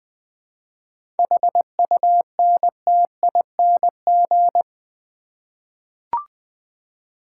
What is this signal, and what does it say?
Morse code, heard as a keyed tone, about 20 wpm, 700 Hz. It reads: HUNTING
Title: 7 Letter Words - First 3 Letters Given - 20wpm